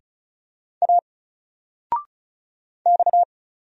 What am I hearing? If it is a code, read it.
Morse code, heard as a keyed tone, about 35 wpm, 700 Hz. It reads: A X
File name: Single-Letter-Number-35wpm